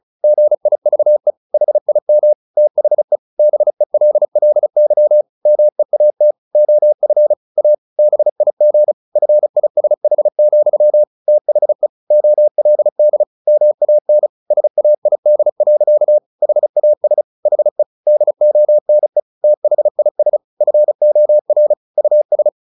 Morse code, heard as a keyed tone, about 34 wpm, 600 Hz. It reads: GIVE HIM THE BELLY MEAT OF A BIG FISH, THE OLD MAN SAID. HAS HE DONE THIS FOR US
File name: the old man and the sea / the old man and the sea 1 35wpm